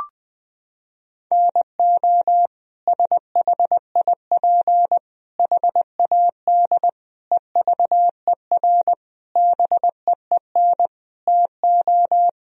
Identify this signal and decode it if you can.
Morse code, heard as a keyed tone, about 20 wpm, 700 Hz. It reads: NO SHIP HAD EVER BEEN TO
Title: Sentences from Top 1000 Words - Review Part 4 - 20wpm